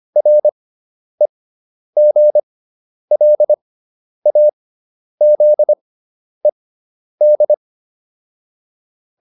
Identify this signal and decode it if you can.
Morse code, heard as a keyed tone, about 25 wpm, spaced as for 10 wpm, 600 Hz. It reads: REGLAZED